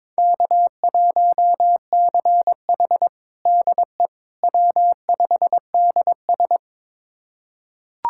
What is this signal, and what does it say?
Morse code, heard as a keyed tone, about 22 wpm, 700 Hz. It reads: K1CH DE W5DS
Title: USA Callsign DE Callsign - 4 Characters - 22wpm